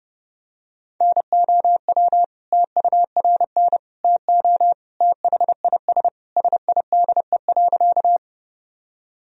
Morse code, heard as a keyed tone, about 32 wpm, 700 Hz. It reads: NOW TURN TO THIS SIDE.
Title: Sentences from Top 1000 Words - Review Part 1 - 30wpm